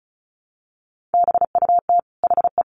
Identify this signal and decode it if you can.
Morse code, heard as a keyed tone, about 34 wpm, 700 Hz. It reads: BUT HE
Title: Sentences from Top 1400 Words 35wpm